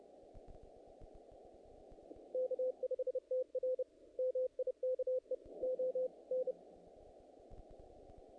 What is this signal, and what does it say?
Morse code, heard as a keyed tone, about 30 wpm, 510 Hz. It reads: K5TR MIKE ON
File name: contests / naqp-cw-k5tr-op-k5pi- -1800